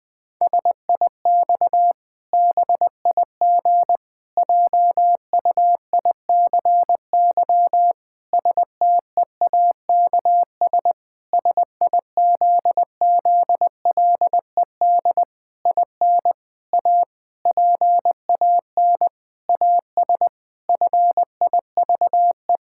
Morse code, heard as a keyed tone, about 20 wpm, 700 Hz. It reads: SIX BIG JUICY STEAKS SIZZLED IN A PAN AS FIVE
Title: Pangram Challenge v2 - 20wpm